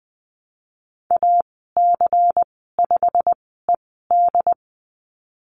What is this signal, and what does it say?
Morse code, heard as a keyed tone, about 20 wpm, spaced as for 15 wpm, 700 Hz. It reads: AC5ED
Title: 179 Lesson 29g - Slash - Call Signs 20wpm